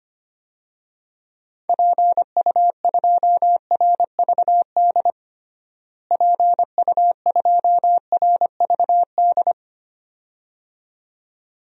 Morse code, heard as a keyed tone, about 25 wpm, 700 Hz. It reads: PU2RVD PU2RVD
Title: DXCC-100 Call Signs - 6 Characters - Repeated 2x - 25wpm